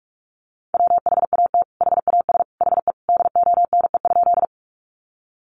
Morse code, heard as a keyed tone, about 44 wpm, 700 Hz. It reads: WHAT HAS HE DONE?
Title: Sentences from Top 500 Words 45wpm